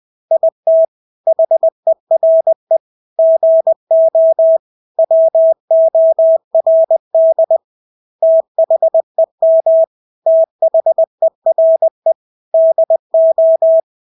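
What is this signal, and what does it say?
Morse code, heard as a keyed tone, about 20 wpm, 650 Hz. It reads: IT HERE GO WORD THEM THERE DO
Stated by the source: Morse practice of English words at 20 wpm.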